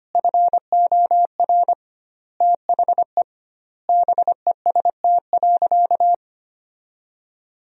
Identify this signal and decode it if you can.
Morse code, heard as a keyed tone, about 25 wpm, 700 Hz. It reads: FOR THE BEST.
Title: 2x Word-Spacing - Sentences from Top 500 Words - 25wpm